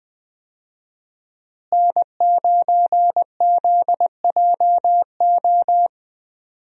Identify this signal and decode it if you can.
Morse code, heard as a keyed tone, about 20 wpm, 700 Hz. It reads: N9ZJO
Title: USA Call Signs - 5 Characters - Repeated 3x - 20wpm